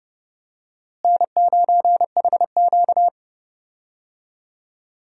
Morse code, heard as a keyed tone, about 30 wpm, 700 Hz. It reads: N9HQ
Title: USA Call Signs - 4 Characters - Repeated 2x - 30wpm